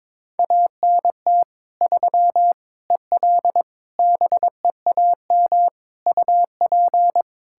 Morse code, heard as a keyed tone, about 22 wpm, 700 Hz. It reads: ANT 3 EL BEAM UP
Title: Practice QSO - Exchange 2 - Info - 22wpm